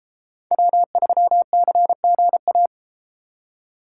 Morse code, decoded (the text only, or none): W3CGA